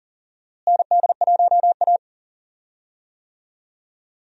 Morse code, heard as a keyed tone, about 40 wpm, 700 Hz. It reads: ND1A
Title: USA Call Signs - 4 Characters - Repeated 3x - 40wpm